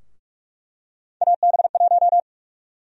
Morse code, heard as a keyed tone, about 46 wpm, 700 Hz. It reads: AB1
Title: USA Call Sign Prefixes - 45wpm